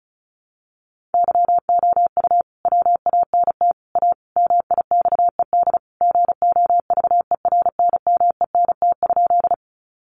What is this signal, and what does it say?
Morse code, heard as a keyed tone, about 34 wpm, 700 Hz. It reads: YOU WANT A MIXED GOVERNMENT?
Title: Sentences from Top 1500 Words 35wpm